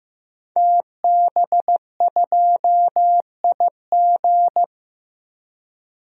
Morse code, heard as a keyed tone, about 15 wpm, 700 Hz. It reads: TB2IG
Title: DXCC-100 Call Signs - 5 Characters - Repeated 3x - 15wpm